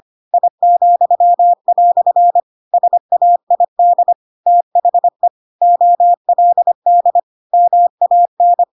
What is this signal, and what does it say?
Morse code, heard as a keyed tone, about 25 wpm, 700 Hz. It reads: I," SAID THE OLD MAN